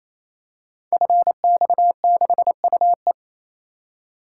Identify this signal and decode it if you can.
Morse code, heard as a keyed tone, about 27 wpm, 700 Hz. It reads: FX6UE
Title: Code Groups - 5 Letter:Numbers - 28wpm